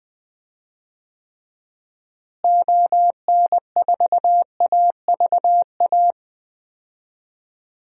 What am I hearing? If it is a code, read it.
Morse code, heard as a keyed tone, about 20 wpm, 700 Hz. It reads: ON4AVA
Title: DXCC-100 Call Signs - 6 Characters - Repeated 3x - 20wpm